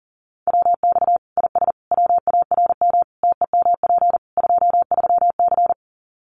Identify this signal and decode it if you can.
Morse code, heard as a keyed tone, about 40 wpm, 700 Hz. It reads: WX IS WARM TEMP 23C
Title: Common QSO Phrases 40wpm